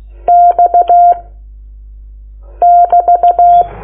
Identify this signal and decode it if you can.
Morse code, heard as a keyed tone, about 16 wpm, 700 Hz. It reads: X <BT>
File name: Lecciones / CLASE Nº9